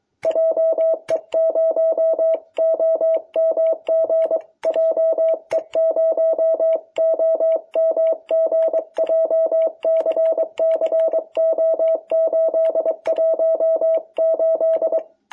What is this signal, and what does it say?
Morse code, heard as a keyed tone, about 23 wpm, 660 Hz. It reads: JE0OMG JE0OMG JCCO818
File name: jh8cbh / 2021cw